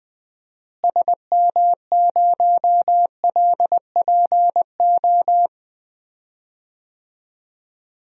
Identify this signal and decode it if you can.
Morse code, heard as a keyed tone, about 21 wpm, 700 Hz. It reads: SM0LPO
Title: DXCC-100 Call Signs - 6 Characters - Repeated 2x - 20wpm